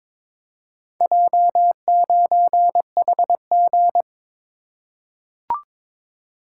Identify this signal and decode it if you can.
Morse code, heard as a keyed tone, about 22 wpm, 700 Hz. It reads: J9HG
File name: practice / 22wpm / Four Letter-Number Combinations - 22wpm